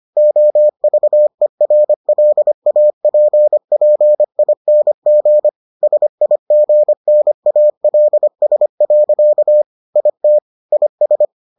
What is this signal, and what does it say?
Morse code, heard as a keyed tone, about 25 wpm, 600 Hz. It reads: OVERLAPPING SIGNALS. IT IS